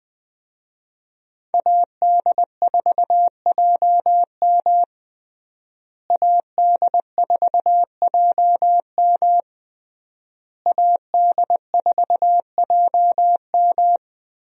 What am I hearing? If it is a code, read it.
Morse code, heard as a keyed tone, about 20 wpm, 700 Hz. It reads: AD4JM AD4JM AD4JM
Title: USA Call Signs - 5 Characters - Repeated 3x - 20wpm